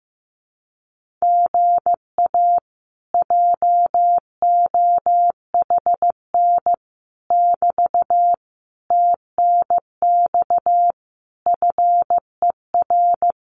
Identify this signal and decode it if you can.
Morse code, heard as a keyed tone, about 15 wpm, 700 Hz. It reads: GA JOHN <BT> TNX FER
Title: Common QSO Phrases Encore 15wpm